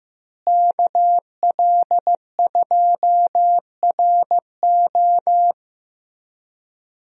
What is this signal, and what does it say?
Morse code, heard as a keyed tone, about 15 wpm, 700 Hz. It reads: KL2RO
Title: USA Call Signs - 5 Characters - Repeated 3x - 15wpm